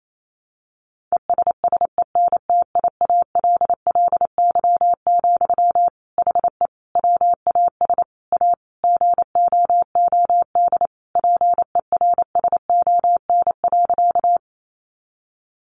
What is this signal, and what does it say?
Morse code, heard as a keyed tone, about 28 wpm, 700 Hz. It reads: ESSENTIALLY, HE WAS A GOOD PERSON.
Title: Sentences from Top 1300 Words 28wpm